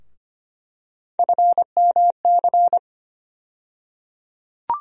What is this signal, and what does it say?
Morse code, heard as a keyed tone, about 25 wpm, 700 Hz. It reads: FMC
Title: USA Call Sign Postfixes - 3 Letters 25wpm